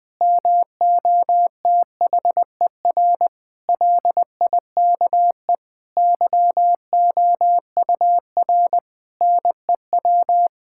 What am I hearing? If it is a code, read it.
Morse code, heard as a keyed tone, about 21 wpm, 700 Hz. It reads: MOTHER LIKE YOUR NEW
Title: Sentences from Top 1000 Words - Review Part 5 - 20wpm